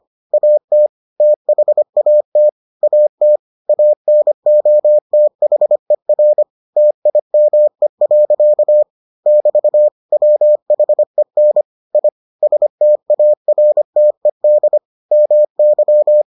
Morse code, heard as a keyed tone, about 25 wpm, 600 Hz. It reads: AT THAT AT ANOTHER TIME. <BT> WHEN I STARTED MY